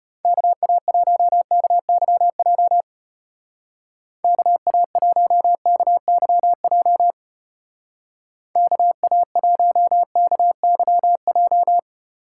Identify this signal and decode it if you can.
Morse code, heard as a keyed tone, about 30 wpm, 700 Hz. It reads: KA1KYJ KA1KYJ KA1KYJ
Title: Speed Racing - USA Call Signs 25wpm